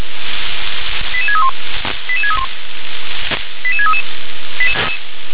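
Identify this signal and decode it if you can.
Five-tone selective call (ZVEI), heard as quick runs of five tones, tones of about 80 ms. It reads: 88631 88631 88633
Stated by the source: ZVEI five-tone selective call received on SDR.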